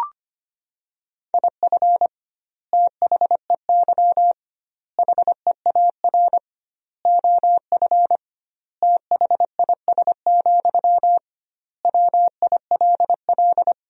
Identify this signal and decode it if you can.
Morse code, heard as a keyed tone, about 25 wpm, 700 Hz. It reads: IF THEY HEAR OF THIS, WILL
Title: 2x Word-Spacing - Sentences from Top 400 Words - 25wpm